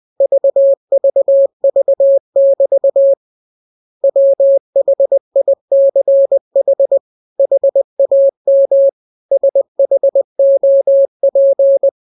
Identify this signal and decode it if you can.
Morse code, heard as a keyed tone, about 20 wpm, 550 Hz. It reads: VVV<BT> WHICH HAM SHOP